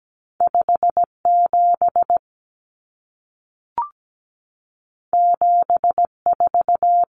Morse code, heard as a keyed tone, about 17 wpm, 700 Hz. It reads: 57 74
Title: Common Cut Numbers - 2-Digit 17wpm